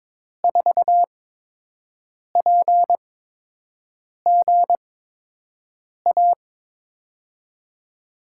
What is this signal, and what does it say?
Morse code, heard as a keyed tone, about 22 wpm, spaced as for 6 wpm, 700 Hz. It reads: 4PGA
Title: USA Call Signs - 6 Characters - 8x Character Spacing - 22wpm